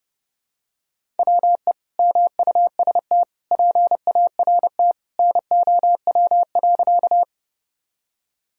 Morse code, heard as a keyed tone, about 32 wpm, 700 Hz. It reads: WE MUST PART NOW.
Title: Sentences from Top 1000 Words - Review Part 1 - 30wpm